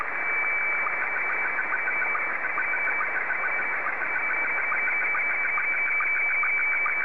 SSTV (slow-scan television), Scottie 1.